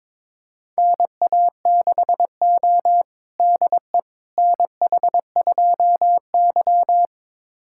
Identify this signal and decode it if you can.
Morse code, heard as a keyed tone, about 22 wpm, 700 Hz. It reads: NA6O DE NH2Y
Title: USA Callsign DE Callsign - 4 Characters - 22wpm